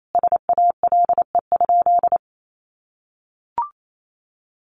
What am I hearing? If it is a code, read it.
Morse code, heard as a keyed tone, about 28 wpm, 700 Hz. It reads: SALE?
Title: Sentences from Top 1400 Words 28wpm